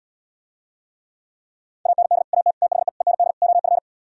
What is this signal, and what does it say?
Morse code, heard as a keyed tone, about 77 wpm, 700 Hz. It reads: AND AT THE END OF